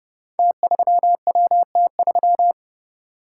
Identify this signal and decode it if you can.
Morse code, heard as a keyed tone, about 32 wpm, 700 Hz. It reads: T3WT3
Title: Code Groups - 5 Letter:Numbers - 30wpm